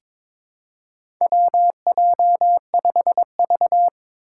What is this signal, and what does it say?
Morse code, heard as a keyed tone, about 22 wpm, 700 Hz. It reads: WJ5V
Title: Four Challenging Letter-Number Combinations 22wpm